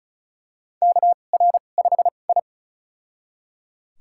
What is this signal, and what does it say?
Morse code, heard as a keyed tone, about 36 wpm, spaced as for 25 wpm, 700 Hz. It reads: KR5I
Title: USA Call Signs - 2x Character Spacing - 35wpm